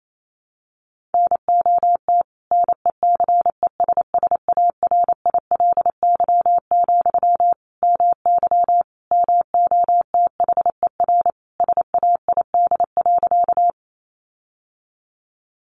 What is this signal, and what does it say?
Morse code, heard as a keyed tone, about 28 wpm, 700 Hz. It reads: NOT NECESSARILY, MY MOTHER SAID.
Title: Sentences from Top 1400 Words 28wpm